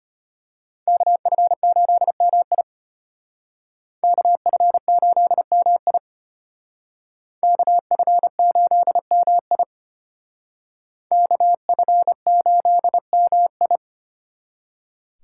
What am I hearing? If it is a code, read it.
Morse code, heard as a keyed tone, about 32 wpm, 700 Hz. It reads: KF8MI KF8MI KF8MI KF8MI